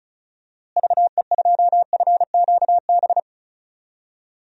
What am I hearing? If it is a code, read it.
Morse code, heard as a keyed tone, about 36 wpm, 700 Hz. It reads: VE2FQB
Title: DXCC-100 Call Signs - 6 Characters - Repeated 2x - 35wpm